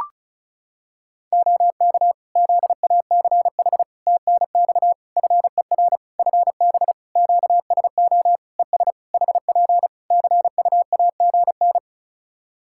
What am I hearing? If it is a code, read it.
Morse code, heard as a keyed tone, about 34 wpm, 700 Hz. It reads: OK ZACH TNX FER FB QSO ES HP CUAGN
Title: K1USN SST QSO Practice 35wpm